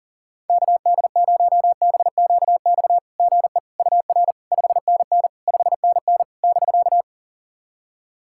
Morse code, heard as a keyed tone, about 40 wpm, 700 Hz. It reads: KD0BQX GE UR 5NN 5NN <BK>
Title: SOTA QSO Practice 40wpm